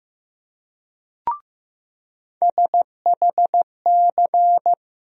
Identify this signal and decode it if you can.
Morse code, heard as a keyed tone, about 15 wpm, 700 Hz. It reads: SHC
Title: USA Call Sign Postfixes - 3 Letters 15wpm